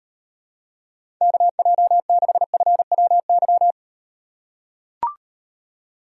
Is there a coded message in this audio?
Yes: Morse code from 1.2 to 3.7 s, heard as a keyed tone, about 38 wpm, 700 Hz. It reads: KJ6FWY